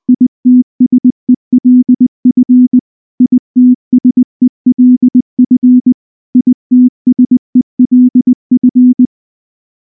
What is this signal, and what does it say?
Morse code, heard as a keyed tone, about 19 wpm, 250 Hz. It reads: ITSELF ITSELF ITSELF
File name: morse / morse-training-20WPM-3